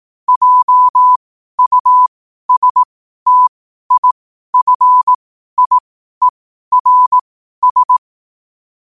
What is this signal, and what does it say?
Morse code, heard as a keyed tone, about 18 wpm, spaced as for 12 wpm, 1000 Hz. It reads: JUSTIFIERS
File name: qotdfiles / 12wpm